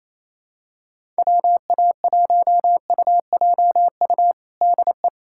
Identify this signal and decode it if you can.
Morse code, heard as a keyed tone, about 28 wpm, 700 Hz. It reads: WA1UJU DE
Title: USA Callsign DE Callsign 28wpm